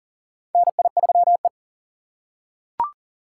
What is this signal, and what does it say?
Morse code, heard as a keyed tone, about 40 wpm, 700 Hz. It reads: NI3E
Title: USA Call Signs - 4 Characters - 4x Character Spacing - 40wpm